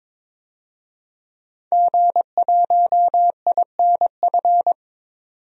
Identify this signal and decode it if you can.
Morse code, heard as a keyed tone, about 22 wpm, 700 Hz. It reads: G1INF
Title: DXCC-100 Call Signs - 5 Characters - 22wpm